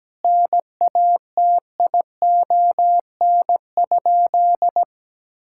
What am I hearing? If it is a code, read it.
Morse code, heard as a keyed tone, about 17 wpm, 700 Hz. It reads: NATION?
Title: Sentences from Top 1000 Words - Review Part 6 - 17wpm